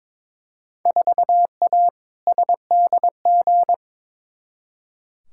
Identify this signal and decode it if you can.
Morse code, heard as a keyed tone, about 21 wpm, 700 Hz. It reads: 4A SDG
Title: Field Day Exchanges Encore - 22wpm